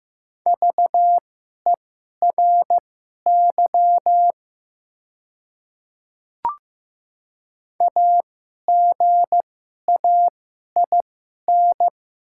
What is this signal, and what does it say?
Morse code, heard as a keyed tone, about 15 wpm, spaced as for 11 wpm, 700 Hz. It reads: VERY AGAIN
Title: 2x Character Spacing - Top 300 Words - 15wpm